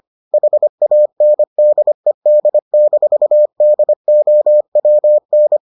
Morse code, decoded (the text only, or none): HANDED-DOWN